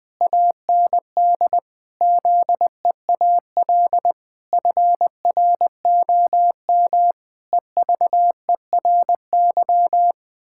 Morse code, heard as a keyed tone, about 21 wpm, 700 Hz. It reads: AND ZEAL FROM EVERY
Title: Pangram Challenge v2 - 20wpm